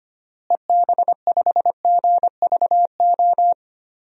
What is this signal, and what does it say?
Morse code, heard as a keyed tone, about 25 wpm, 700 Hz. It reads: EB5GVO
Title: DXCC-100 Call Signs - 6 Characters - Repeated 3x - 25wpm